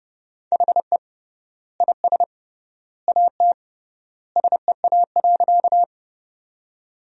Morse code, heard as a keyed tone, about 29 wpm, 700 Hz. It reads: HE IS AT SEA.
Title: 3x Word-Spacing - Sentences from Top 300 Words - 30wpm